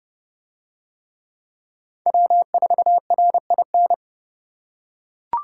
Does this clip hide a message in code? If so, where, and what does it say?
Yes: Morse code from 2.1 to 3.9 s, heard as a keyed tone, about 28 wpm, 700 Hz. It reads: W4RIN